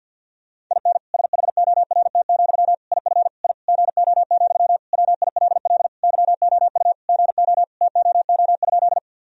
Morse code, heard as a keyed tone, about 46 wpm, 700 Hz. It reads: IN SHORT, IF I GO, WILL YOU GO TOO?
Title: Sentences from Top 1000 Words - Review Part 2 - 50wpm